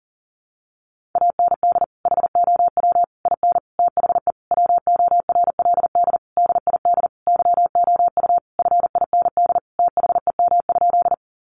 Morse code, heard as a keyed tone, about 40 wpm, 700 Hz. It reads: AND HOW IN THE WORLD DID YOU FIND THEM?
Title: Sentences from Top 200 Words 40wpm